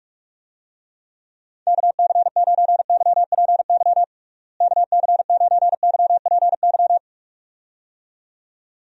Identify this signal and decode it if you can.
Morse code, heard as a keyed tone, about 46 wpm, 700 Hz. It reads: KC9YPY KC9YPY
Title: USA Call Signs - 6 Characters - Repeated 2x - 45wpm